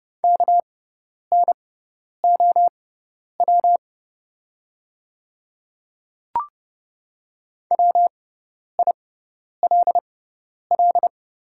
Morse code, heard as a keyed tone, about 30 wpm, spaced as for 10 wpm, 700 Hz. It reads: KNOW WILL
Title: 6x Character Spacing - Top 100 Words - 30wpm